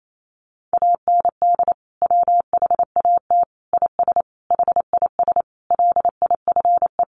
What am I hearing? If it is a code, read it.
Morse code, heard as a keyed tone, about 28 wpm, 700 Hz. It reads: AND WHAT IS HIS LIFE
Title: Sentences from Top 800 Words 28wpm